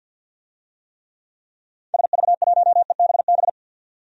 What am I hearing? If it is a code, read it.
Morse code, heard as a keyed tone, about 51 wpm, 700 Hz. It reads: SV1EBB
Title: DXCC-100 Call Signs - 6 Characters - 50wpm